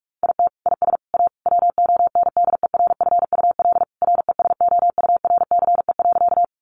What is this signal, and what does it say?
Morse code, heard as a keyed tone, about 44 wpm, 700 Hz. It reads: IT IS A WONDERFUL RESOURCE.
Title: Sentences from Top 1300 Words 45wpm